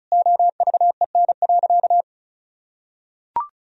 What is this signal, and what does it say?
Morse code, heard as a keyed tone, about 36 wpm, 700 Hz. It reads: OVEN.
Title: Sentences from Top 1000 Words - Review Part 4 - 35wpm